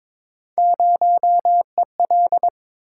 Morse code, heard as a keyed tone, about 22 wpm, 700 Hz. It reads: 0EL
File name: practice / 22wpm / Three Letter-Number Combinations - 22wpm